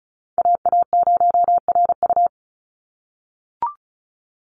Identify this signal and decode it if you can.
Morse code, heard as a keyed tone, about 34 wpm, 700 Hz. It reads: AA0RU
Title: USA Call Signs 35wpm